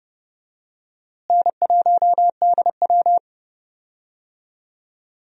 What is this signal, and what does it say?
Morse code, heard as a keyed tone, about 30 wpm, 700 Hz. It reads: N1DW
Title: USA Call Signs - 4 Characters - Repeated 2x - 30wpm